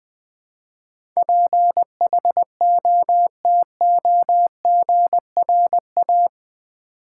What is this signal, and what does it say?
Morse code, heard as a keyed tone, about 20 wpm, 700 Hz. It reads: PHOTOGRA
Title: Build A Word - 4 Letters At A Time - 12-Letter Words - 20wpm